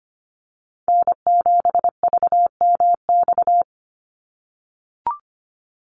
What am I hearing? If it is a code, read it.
Morse code, heard as a keyed tone, about 25 wpm, 700 Hz. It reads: N7VMX
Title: USA Call Signs 25wpm